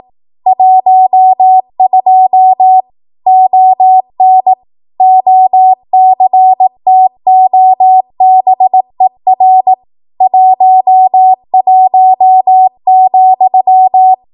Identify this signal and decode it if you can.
Morse code, heard as a keyed tone, about 18 wpm, 750 Hz. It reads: 12 ON OCTOBER 11,
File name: Archive / 18 WPM / 18WPM